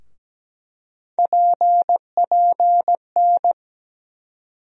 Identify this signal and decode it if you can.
Morse code, heard as a keyed tone, about 17 wpm, 700 Hz. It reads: PPN
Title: USA Call Sign Postfixes - 3 Letters 17wpm